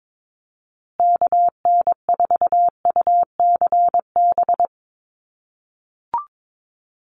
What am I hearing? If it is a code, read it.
Morse code, heard as a keyed tone, about 22 wpm, 700 Hz. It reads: KN4UCB
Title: USA Call Signs Repeated 2x 22wpm